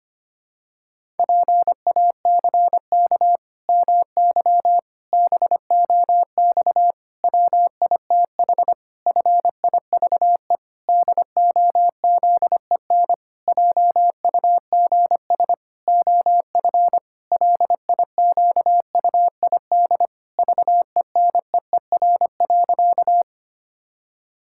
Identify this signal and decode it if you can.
Morse code, heard as a keyed tone, about 25 wpm, 700 Hz. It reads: PACK MY BOX WITH FIVE DOZEN JUGS OF LIQUID VENEER.
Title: Pangram Challenge v2 - 25wpm